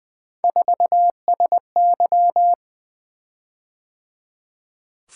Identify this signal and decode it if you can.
Morse code, heard as a keyed tone, about 20 wpm, 700 Hz. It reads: 4SY
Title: Three Letter-Number Combinations - 20wpm